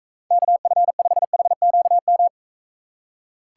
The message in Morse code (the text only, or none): KF5HQM